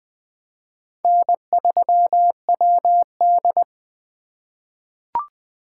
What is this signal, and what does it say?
Morse code, heard as a keyed tone, about 20 wpm, 700 Hz. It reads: N3WD
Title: 2023 POTA Activator Call Signs - 20wpm